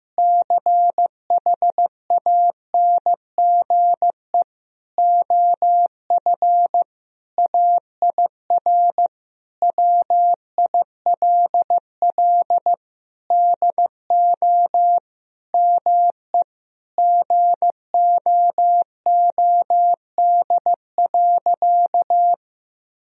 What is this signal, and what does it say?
Morse code, heard as a keyed tone, about 15 wpm, 700 Hz. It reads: CHANGE OF AIR WILL DO ME GOOD.
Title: Sentences from Top 1000 Words - Review Part 1 - 15wpm